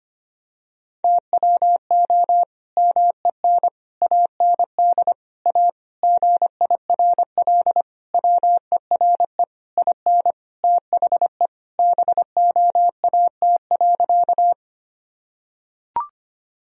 Morse code, heard as a keyed tone, about 25 wpm, 700 Hz. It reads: TWO MEN AND A GIRL WERE IN THE BOAT.
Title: Sentences from Top 1000 Words - Review Part 3 - 25wpm